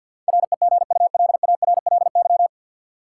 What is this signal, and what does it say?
Morse code, heard as a keyed tone, about 51 wpm, 700 Hz. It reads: REGULARLY
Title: Sentences from Top 1400 Words - New Word Drill - 50wpm